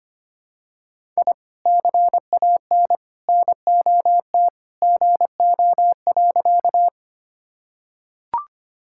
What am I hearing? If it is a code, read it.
Morse code, heard as a keyed tone, about 25 wpm, 700 Hz. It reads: I CAN NOT GO.
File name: practice / 25wpm / Sentences with 2-Word Phrases - 25wpm